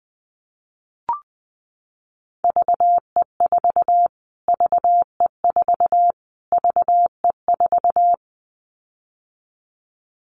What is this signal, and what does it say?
Morse code, heard as a keyed tone, about 20 wpm, 700 Hz. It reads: VE4 VE4 VE4
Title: Callsign Prefixes for Top 25 DXCC - Repeated 3x - 20wpm